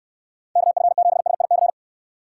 Morse code, heard as a keyed tone, about 69 wpm, 700 Hz. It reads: KF7SIC